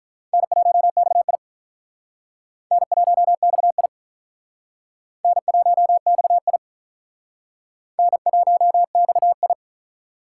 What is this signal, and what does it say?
Morse code, heard as a keyed tone, about 46 wpm, 700 Hz. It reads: N1XI N1XI N1XI N1XI